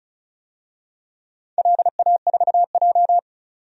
Morse code, heard as a keyed tone, about 34 wpm, 700 Hz. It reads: LA4J
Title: DXCC-100 Call Signs - 4 Characters - 35wpm